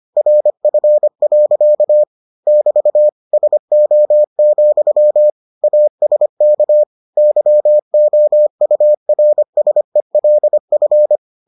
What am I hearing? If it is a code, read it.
Morse code, heard as a keyed tone, about 25 wpm, 600 Hz. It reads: RF. <BT> SO, ASK YOURSELF